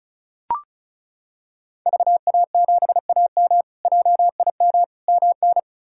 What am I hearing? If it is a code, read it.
Morse code, heard as a keyed tone, about 36 wpm, 700 Hz. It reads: VA7AM JIM MN
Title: K1USN SST QSO Practice - 35wpm